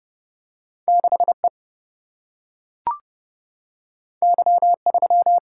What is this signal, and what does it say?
Morse code, heard as a keyed tone, about 30 wpm, 700 Hz. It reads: 6E Y3
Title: Single-Letter Single-Number 30wpm